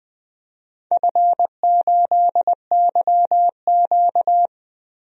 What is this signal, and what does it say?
Morse code, heard as a keyed tone, about 20 wpm, 700 Hz. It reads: F8YQ
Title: DXCC-100 Call Signs - 4 Characters - 20wpm